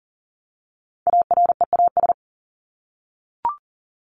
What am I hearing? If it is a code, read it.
Morse code, heard as a keyed tone, about 40 wpm, 700 Hz. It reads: AREAS
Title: ES:S Suffix - Top 100 Words with Base Words 40wpm